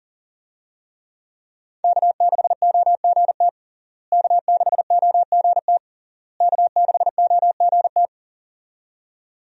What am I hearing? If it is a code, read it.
Morse code, heard as a keyed tone, about 40 wpm, 700 Hz. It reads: K6OGT K6OGT K6OGT
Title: USA Call Signs - 5 Characters - Repeated 3x - 40wpm